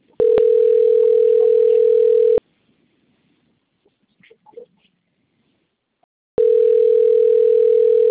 Ringback tone (North America), 440 + 480 Hz.